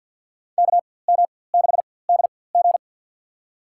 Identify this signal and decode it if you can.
Morse code, heard as a keyed tone, about 50 wpm, spaced as for 23 wpm, 700 Hz. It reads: KM6DG
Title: USA Call Signs - 5 Characters - 4x Character Spacing - 50wpm